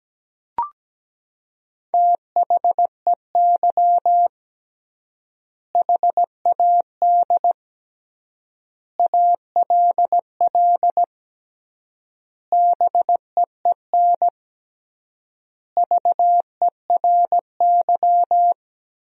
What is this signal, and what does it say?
Morse code, heard as a keyed tone, about 17 wpm, 700 Hz. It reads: THEY HAD ALL BEEN VERY
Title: 3x Word-Spacing - Sentences from Top 200 Words - 17wpm